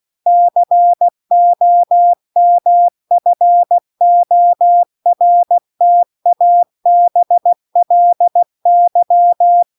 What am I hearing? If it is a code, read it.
Morse code, heard as a keyed tone, about 16 wpm, 700 Hz. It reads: COMFORTABLY